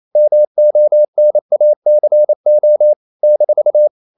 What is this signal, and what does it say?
Morse code, heard as a keyed tone, about 28 wpm, 600 Hz. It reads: MONACO -
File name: CW / Practice / NationalCapitals28